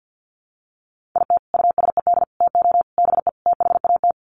Morse code, heard as a keyed tone, about 50 wpm, 700 Hz. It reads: IT USED TO BE THAT
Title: Sets of 5 Words Encore2 50wpm